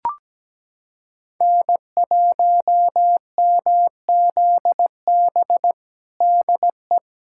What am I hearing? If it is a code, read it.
Morse code, heard as a keyed tone, about 17 wpm, 700 Hz. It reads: N1MZB DE
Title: USA Callsign DE Callsign - 5 Characters - 17wpm